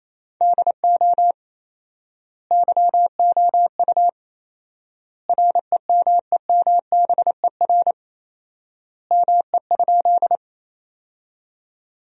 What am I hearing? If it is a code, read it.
Morse code, heard as a keyed tone, about 28 wpm, 700 Hz. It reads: DO YOU REMEMBER ME?
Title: 4x Word-Spacing - Sentences from Top 400 Words - 28wpm